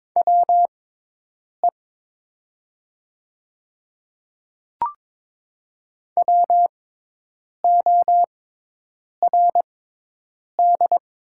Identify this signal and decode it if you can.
Morse code, heard as a keyed tone, about 22 wpm, spaced as for 8 wpm, 700 Hz. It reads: WE WORD
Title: 6x Character Spacing - Top 100 Words - 22wpm